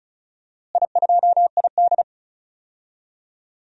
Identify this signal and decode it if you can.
Morse code, heard as a keyed tone, about 34 wpm, 700 Hz. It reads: I2ID
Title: DXCC-100 Call Signs - 4 Characters - Repeated 3x - 35wpm